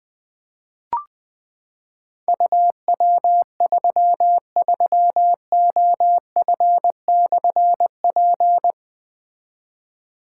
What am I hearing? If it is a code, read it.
Morse code, heard as a keyed tone, about 20 wpm, 700 Hz. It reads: UW33OF/P
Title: Difficult Call Signs 20wpm